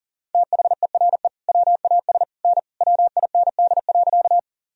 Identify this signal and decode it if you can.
Morse code, heard as a keyed tone, about 40 wpm, 700 Hz. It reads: THERE WAS N WIND.
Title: Sentences from Top 400 Words and Abbreviations 40wpm